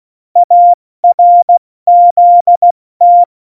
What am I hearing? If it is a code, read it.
Morse code, heard as a keyed tone, about 16 wpm, 700 Hz. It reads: ARZT
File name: Morsekurs / Witze / Witz 13 March 2018, 70 F80